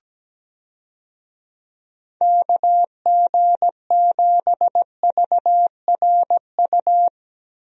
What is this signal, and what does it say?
Morse code, heard as a keyed tone, about 17 wpm, 700 Hz. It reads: KG7VRU
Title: USA Call Signs - 6 Characters - Repeated 3x - 17wpm